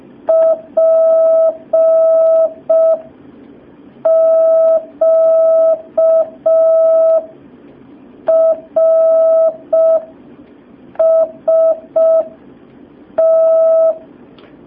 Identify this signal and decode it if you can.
Morse code, heard as a keyed tone, about 5 wpm, spaced as for 4 wpm, 650 Hz. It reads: PQRST
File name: jh8cbh / 2019cw